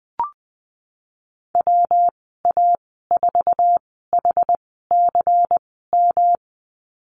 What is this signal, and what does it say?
Morse code, heard as a keyed tone, about 20 wpm, spaced as for 14 wpm, 700 Hz. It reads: WA4HCM
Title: 161 Lesson 26g - Letter g - Call Signs 20wpm